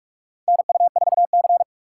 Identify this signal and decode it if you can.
Morse code, heard as a keyed tone, about 41 wpm, 690 Hz. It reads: NU4C
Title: USA Call Signs - 4 Characters - 8x Character Spacing - 45wpm